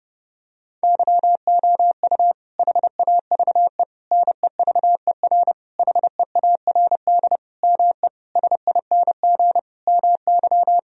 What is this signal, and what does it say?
Morse code, heard as a keyed tone, about 30 wpm, 700 Hz. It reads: YOU HAVE NEVER HEARD ME SING MY
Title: Sentences from Top 1000 Words - Review Part 2 - 30wpm